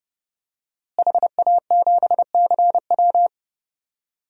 Morse code, heard as a keyed tone, about 31 wpm, 700 Hz. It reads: HA7CW
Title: DXCC-100 Call Signs - 5 Characters - Repeated 3x - 30wpm